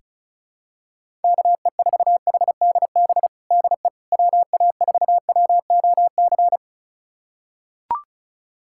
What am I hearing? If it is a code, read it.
Morse code, heard as a keyed tone, about 35 wpm, 700 Hz. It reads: KE4HDB DE WA4WOC
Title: USA Callsign DE Callsign - 6 Characters - 35wpm